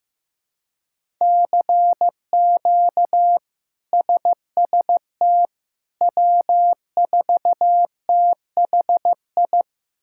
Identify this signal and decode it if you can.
Morse code, heard as a keyed tone, about 16 wpm, 700 Hz. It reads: CQ SST W4THI
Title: K1USN SST QSO Practice - 15wpm